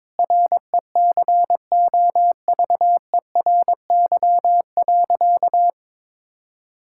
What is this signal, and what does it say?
Morse code, heard as a keyed tone, about 22 wpm, 700 Hz. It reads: RECOVERY.
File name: Sentences with 7-Word Phrases - 22wpm